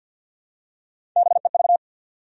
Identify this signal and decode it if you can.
Morse code, heard as a keyed tone, about 50 wpm, 700 Hz. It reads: BEV
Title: USA Common Names - Max 3 Letters - 50wpm